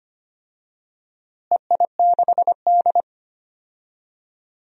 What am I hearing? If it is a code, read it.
Morse code, heard as a keyed tone, about 25 wpm, 700 Hz. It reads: EI6D
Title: DXCC-100 Call Signs - 4 Characters - Repeated 2x - 25wpm